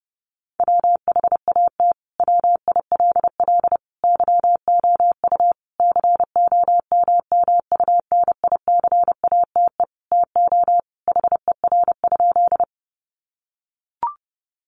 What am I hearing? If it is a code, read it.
Morse code, heard as a keyed tone, about 28 wpm, 700 Hz. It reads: WHAT WILL YOU COMMUNICATE TO HER?
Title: Sentences from Top 1300 Words 30wpm